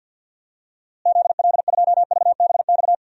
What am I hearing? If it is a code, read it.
Morse code, heard as a keyed tone, about 52 wpm, 700 Hz. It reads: ZL2VBX
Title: DXCC-100 Call Signs - 6 Characters - Repeated 2x - 50wpm